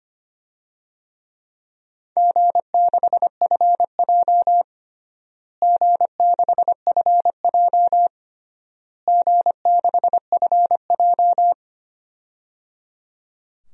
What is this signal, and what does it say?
Morse code, heard as a keyed tone, about 25 wpm, 700 Hz. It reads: G6FJ G6FJ G6FJ
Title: DXCC-100 Call Signs - 4 Characters - Repeated 3x - 25wpm